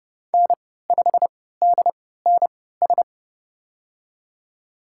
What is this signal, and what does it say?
Morse code, heard as a keyed tone, about 30 wpm, spaced as for 17 wpm, 700 Hz. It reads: N5DNS
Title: USA Call Signs - 5 Characters - 3x Character Spacing - 30wpm